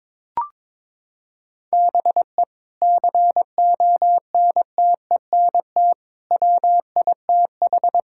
Morse code, heard as a keyed tone, about 22 wpm, 700 Hz. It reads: BE CONTENT WITH
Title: Sentences from Top 1000 Words - Review Part 6 - 22wpm